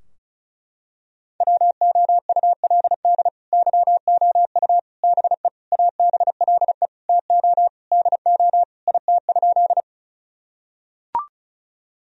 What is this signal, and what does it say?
Morse code, heard as a keyed tone, about 34 wpm, 700 Hz. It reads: WOULD YOU BE ABLE TO DO IT?
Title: Sentences from Top 1000 Words - Review Part 2 - 35wpm